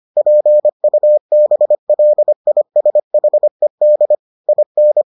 Morse code, heard as a keyed tone, about 25 wpm, 600 Hz. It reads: PUBLISHED IN